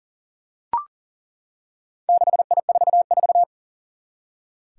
Morse code, heard as a keyed tone, about 39 wpm, 700 Hz. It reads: 6I44
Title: Four Challenging Letter-Number Combinations 40wpm